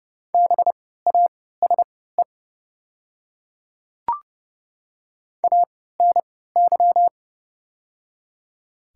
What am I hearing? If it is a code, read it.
Morse code, heard as a keyed tone, about 30 wpm, spaced as for 17 wpm, 700 Hz. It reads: BASE ANY
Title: 3x Character Spacing - Top 300 Words - 30wpm